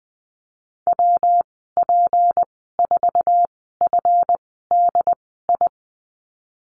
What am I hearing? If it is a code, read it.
Morse code, heard as a keyed tone, about 20 wpm, spaced as for 15 wpm, 700 Hz. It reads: WP4FDI
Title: 154 Lesson 25g - Letter p - Call Signs 20wpm